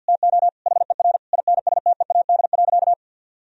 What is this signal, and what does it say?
Morse code, heard as a keyed tone, about 50 wpm, 700 Hz. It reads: TO HER INSTEAD.
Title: Sentences from Top 1000 Words - Review Part 4 - 50wpm